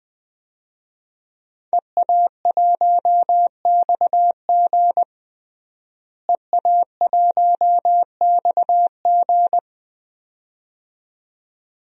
Morse code, heard as a keyed tone, about 20 wpm, 700 Hz. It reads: EA1XG EA1XG
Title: DXCC-100 Call Signs - 5 Characters - Repeated 2x - 20wpm